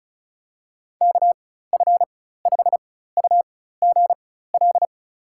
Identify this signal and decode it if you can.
Morse code, heard as a keyed tone, about 36 wpm, spaced as for 16 wpm, 700 Hz. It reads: KF5UGL
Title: USA Call Signs - 4x Character Spacing - 35wpm